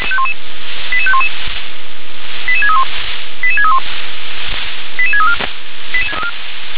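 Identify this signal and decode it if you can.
Five-tone selective call (ZVEI), heard as quick runs of five tones, tones of about 70 ms. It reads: 88511 88631 88631 88634
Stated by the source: ZVEI five-tone call received by SDR.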